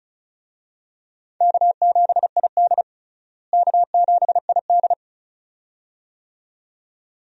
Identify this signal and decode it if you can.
Morse code, heard as a keyed tone, about 34 wpm, 700 Hz. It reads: K7ID K7ID
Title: USA Call Signs - 4 Characters - Repeated 2x - 35wpm